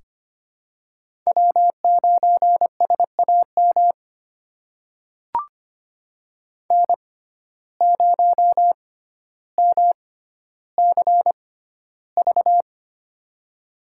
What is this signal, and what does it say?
Morse code, heard as a keyed tone, about 25 wpm, spaced as for 13 wpm, 700 Hz. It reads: W9SAM N0MCV
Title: USA Call Signs - 5 Characters - 6x Character Spacing - 25wpm